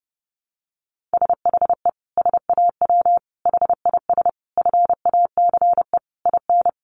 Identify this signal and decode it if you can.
Morse code, heard as a keyed tone, about 30 wpm, 700 Hz. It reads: SHE SAW HIS FACE IN
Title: Sentences from Top 1300 Words 30wpm